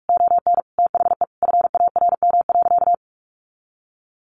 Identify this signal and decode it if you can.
Morse code, heard as a keyed tone, about 44 wpm, 700 Hz. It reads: ON THE FARM.
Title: Sentences from Top 1200 Words 45wpm